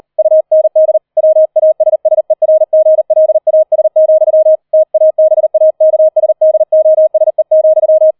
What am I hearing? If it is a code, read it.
Morse code, heard as a keyed tone, about 38 wpm, 630 Hz. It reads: UND WASSERGLAS, TABAKSDOSE,
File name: 004 Vierter Streich 40wpm